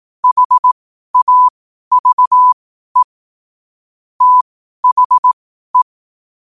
Morse code, heard as a keyed tone, about 18 wpm, spaced as for 12 wpm, 1000 Hz. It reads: HAVE THE
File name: qotdfiles / 12wpm